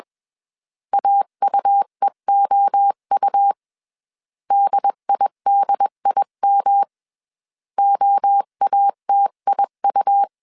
Morse code, heard as a keyed tone, about 21 wpm, 800 Hz. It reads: AUEOU DIDIM OATIU
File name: 20wpm